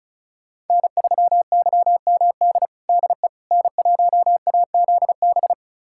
Morse code, heard as a keyed tone, about 36 wpm, 690 Hz. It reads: N3YMD DE N1AZB
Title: USA Callsign DE Callsign - 5 Characters - 35wpm